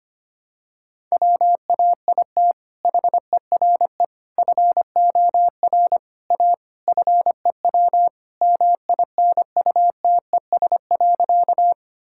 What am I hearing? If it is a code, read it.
Morse code, heard as a keyed tone, about 25 wpm, 700 Hz. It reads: WAIT HERE FOR A FEW MINUTES.
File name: practice / 25wpm / Sentences with 3-Word Phrases - 25wpm